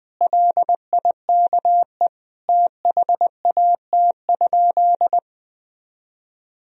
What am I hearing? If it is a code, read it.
Morse code, heard as a keyed tone, about 21 wpm, 700 Hz. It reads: LIKE THAT?
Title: Sentences from Top 1000 Words - Review Part 1 - 20wpm